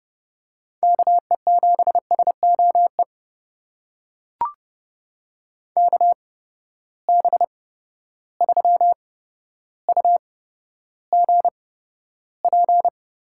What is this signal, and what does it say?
Morse code, heard as a keyed tone, about 30 wpm, spaced as for 13 wpm, 700 Hz. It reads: KE7SOE KB3UGP
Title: USA Call Signs - 6 Characters - 8x Character Spacing - 30wpm